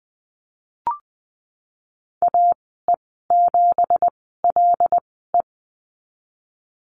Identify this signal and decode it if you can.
Morse code, heard as a keyed tone, about 20 wpm, spaced as for 15 wpm, 700 Hz. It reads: AE7LE
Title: 179 Lesson 29g - Slash - Call Signs 20wpm